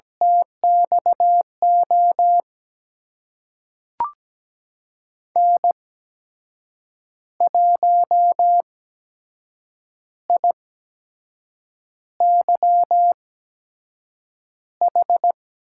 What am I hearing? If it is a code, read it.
Morse code, heard as a keyed tone, about 17 wpm, spaced as for 6 wpm, 700 Hz. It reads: TXO N1IYH
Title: USA Call Signs - 5 Characters - 8x Character Spacing - 17wpm